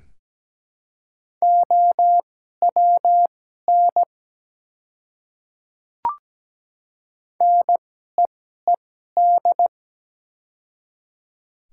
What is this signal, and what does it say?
Morse code, heard as a keyed tone, about 17 wpm, spaced as for 12 wpm, 700 Hz. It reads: OWN NEED
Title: 2x Character Spacing - Top 300 Words - 17wpm